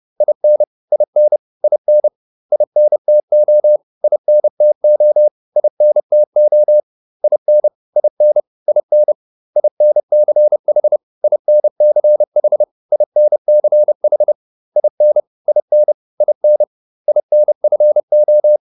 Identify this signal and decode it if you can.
Morse code, heard as a keyed tone, about 30 wpm, 600 Hz. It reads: IN IN IN INTO INTO INTO IN IN IN INCH INCH INCH IN IN IN INFO